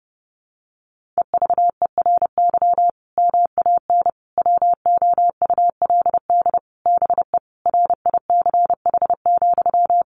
Morse code, heard as a keyed tone, about 30 wpm, 700 Hz. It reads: EVERY MAN WOULD BE RICH,
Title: Sentences from Top 1500 Words 30wpm